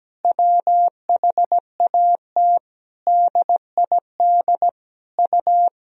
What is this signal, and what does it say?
Morse code, heard as a keyed tone, about 17 wpm, 700 Hz. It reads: WHAT DID U
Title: Sentences from Top 200 Words and Abbreviations 17wpm